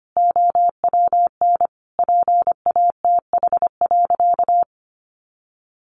Morse code, heard as a keyed tone, about 25 wpm, 700 Hz. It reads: OWN PATH.
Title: Sentences from Top 1400 Words 25wpm